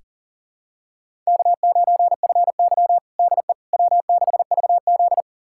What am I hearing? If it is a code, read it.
Morse code, heard as a keyed tone, about 40 wpm, 700 Hz. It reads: K9FY DE W6VZ
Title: USA Callsign DE Callsign - 4 Characters - 40wpm